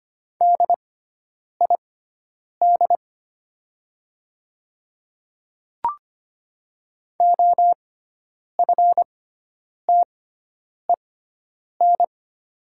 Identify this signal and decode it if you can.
Morse code, heard as a keyed tone, about 25 wpm, spaced as for 9 wpm, 700 Hz. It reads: DID OFTEN
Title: 6x Character Spacing - Top 300 Words - 25wpm